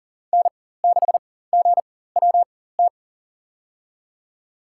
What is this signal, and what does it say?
Morse code, heard as a keyed tone, about 40 wpm, spaced as for 19 wpm, 700 Hz. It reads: N6GWT
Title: USA Call Signs - 5 Characters - 4x Character Spacing - 40wpm